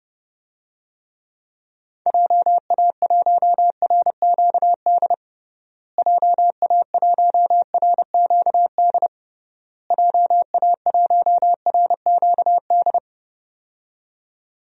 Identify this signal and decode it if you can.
Morse code, heard as a keyed tone, about 30 wpm, 700 Hz. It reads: JA1RQD JA1RQD JA1RQD
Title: DXCC-100 Call Signs - 6 Characters - Repeated 3x - 30wpm